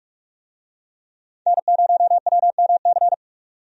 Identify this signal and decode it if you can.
Morse code, heard as a keyed tone, about 43 wpm, 700 Hz. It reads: N0WMC
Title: USA Call Signs - 5 Characters - 45wpm